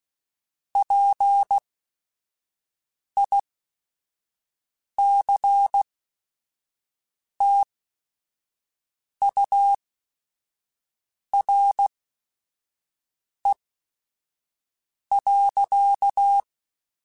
Morse code, heard as a keyed tone, about 16 wpm, spaced as for 5 wpm, 800 Hz. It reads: PICTURE.
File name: Technical Subjects / Technical Subjects-mp3s / AND Gate @5